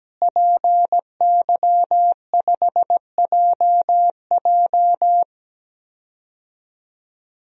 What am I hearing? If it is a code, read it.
Morse code, heard as a keyed tone, about 17 wpm, 700 Hz. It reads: PY5JJ
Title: DXCC-100 Call Signs - 5 Characters - Repeated 2x - 17wpm